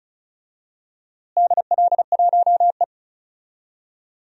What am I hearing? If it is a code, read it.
Morse code, heard as a keyed tone, about 34 wpm, 700 Hz. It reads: DL1E
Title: DXCC-100 Call Signs - 4 Characters - Repeated 2x - 35wpm